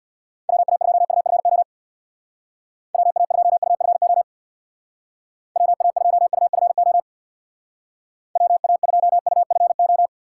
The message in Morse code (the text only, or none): WA2UFO WA2UFO WA2UFO WA2UFO